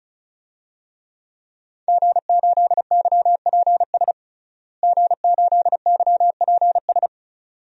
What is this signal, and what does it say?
Morse code, heard as a keyed tone, about 36 wpm, 700 Hz. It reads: G8YPS G8YPS
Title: DXCC-100 Call Signs - 5 Characters - Repeated 2x - 35wpm